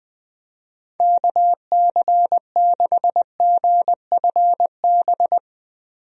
Morse code, heard as a keyed tone, about 21 wpm, 700 Hz. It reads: KC6GFB
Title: USA Call Signs - 6 Characters - 20wpm